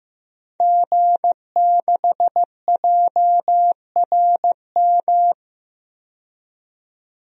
Morse code, heard as a keyed tone, about 15 wpm, 700 Hz. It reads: G6JRM
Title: DXCC-100 Call Signs - 5 Characters - 15wpm